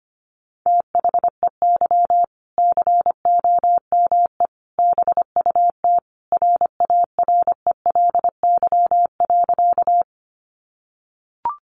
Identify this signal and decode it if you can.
Morse code, heard as a keyed tone, about 25 wpm, 700 Hz. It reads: THEY COME BUT RARELY.
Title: Sentences from Top 1400 Words 25wpm